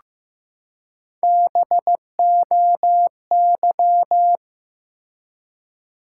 Morse code, heard as a keyed tone, about 15 wpm, 700 Hz. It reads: BOY
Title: Three Letter Words - 15wpm